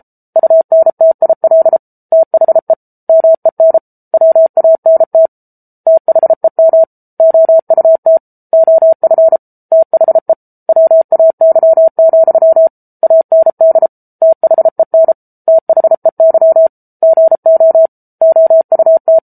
Morse code, heard as a keyed tone, about 33 wpm, 650 Hz. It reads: UNTIL THE MEN WANT THEM OUT OF THE WAY, AND THEN THEY GO OUT